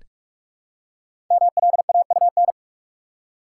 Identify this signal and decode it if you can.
Morse code, heard as a keyed tone, about 44 wpm, 700 Hz. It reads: MLAUN
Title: Code Groups - 5 Letters - 45wpm